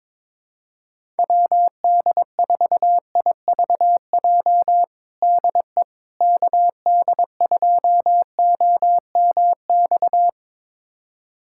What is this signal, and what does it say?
Morse code, heard as a keyed tone, about 22 wpm, 700 Hz. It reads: WD4IVJ DE KD2OMX
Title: USA Callsign DE Callsign 22wpm